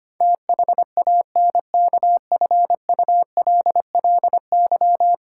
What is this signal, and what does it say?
Morse code, heard as a keyed tone, about 25 wpm, 700 Hz. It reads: THANKFULLY
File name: 10 Letter Words - First 3 Letters Given - 25wpm